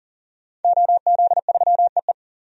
Morse code, heard as a keyed tone, about 40 wpm, 700 Hz. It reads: OZ3EE
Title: DXCC-100 Call Signs - 5 Characters - 40wpm